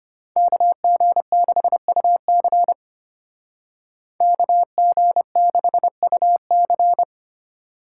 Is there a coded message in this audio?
Yes: Morse code from 0.4 to 7.0 s, heard as a keyed tone, about 33 wpm, 700 Hz. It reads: KG6UC KG6UC